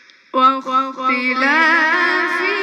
There is no code.